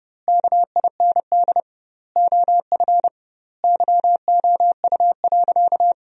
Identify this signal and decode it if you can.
Morse code, heard as a keyed tone, about 30 wpm, 700 Hz. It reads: KIND OF YOU.
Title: 2x Word-Spacing - Sentences from Top 200 Words - 30wpm